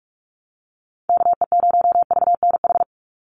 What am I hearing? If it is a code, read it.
Morse code, heard as a keyed tone, about 46 wpm, 690 Hz. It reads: KE0VNH
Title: USA Call Signs 45wpm